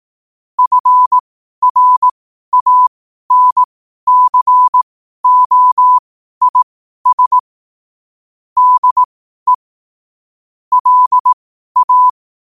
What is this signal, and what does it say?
Morse code, heard as a keyed tone, about 18 wpm, spaced as for 12 wpm, 1000 Hz. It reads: FRANCOIS DE LA
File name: qotdfiles / 12wpm